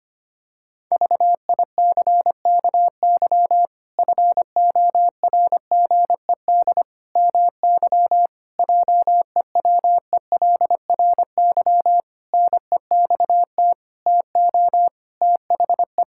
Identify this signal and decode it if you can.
Morse code, heard as a keyed tone, about 25 wpm, 700 Hz. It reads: VICKY FORGED MY JEWELRY NEXT TO THE
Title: Pangram Challenge v2 - 25wpm